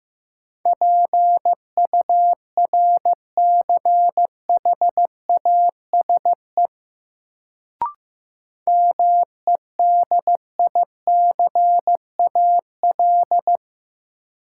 Morse code, heard as a keyed tone, about 15 wpm, 700 Hz. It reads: PURCHASE MEDICAL
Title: Sentences from Top 800 Words - New Word Drill - 15wpm